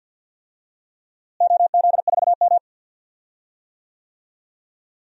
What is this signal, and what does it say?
Morse code, heard as a keyed tone, about 52 wpm, 700 Hz. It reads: OZ4M
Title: DXCC-100 Call Signs - 4 Characters - Repeated 2x - 50wpm